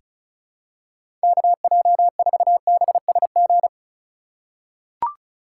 Morse code, heard as a keyed tone, about 36 wpm, 700 Hz. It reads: KJ4BSG
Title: USA Call Signs - 6 Characters - 3x Character Spacing - 35wpm